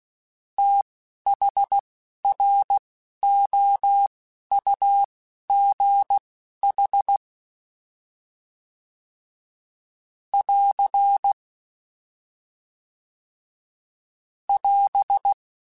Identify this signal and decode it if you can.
Morse code, heard as a keyed tone, about 16 wpm, spaced as for 11 wpm, 800 Hz. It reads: THROUGH <AR> <AS>